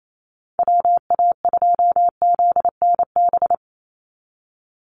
Morse code, heard as a keyed tone, about 28 wpm, 700 Hz. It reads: WA2ZNB
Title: USA Call Signs 28wpm